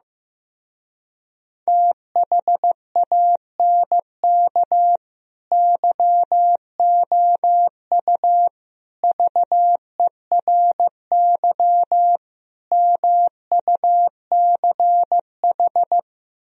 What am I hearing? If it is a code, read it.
Morse code, heard as a keyed tone, about 15 wpm, 700 Hz. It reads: THANK YOU VERY MUCH
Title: Speed-Racing - Sets of 4 Words 15wpm